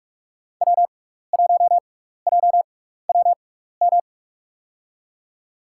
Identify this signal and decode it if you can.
Morse code, heard as a keyed tone, about 46 wpm, spaced as for 15 wpm, 700 Hz. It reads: W1JWM
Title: USA Call Signs - 5 Characters - 6x Character Spacing - 45wpm